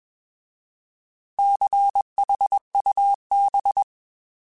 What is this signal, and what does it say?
Morse code, heard as a keyed tone, about 21 wpm, 800 Hz. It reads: CHUB